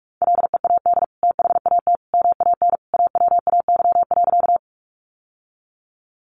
Morse code, heard as a keyed tone, about 44 wpm, 700 Hz. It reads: LEAD THAT MAN AWAY.
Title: Sentences from Top 600 Words 45wpm